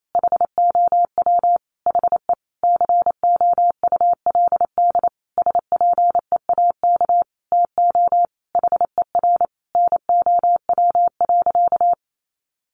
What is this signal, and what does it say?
Morse code, heard as a keyed tone, about 28 wpm, 700 Hz. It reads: HOW HE COULD SPEAK TO HER NOW.
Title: Sentences from Top 1400 Words 28wpm